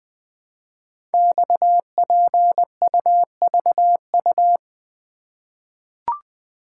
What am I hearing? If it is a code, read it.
Morse code, heard as a keyed tone, about 20 wpm, 700 Hz. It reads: XPUVU
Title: Code Groups - 5 Letters - 20wpm